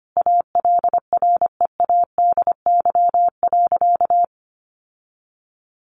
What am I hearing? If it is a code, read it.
Morse code, heard as a keyed tone, about 25 wpm, 700 Hz. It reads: ALREADY.
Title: Sentences from Top 1000 Words 25wpm